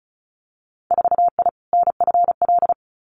Morse code, heard as a keyed tone, about 34 wpm, 700 Hz. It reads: 4I NFL
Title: Winter Field Day Exchanges 35wpm